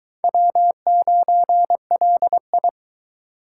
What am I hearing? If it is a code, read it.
Morse code, heard as a keyed tone, about 22 wpm, 700 Hz. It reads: W9LI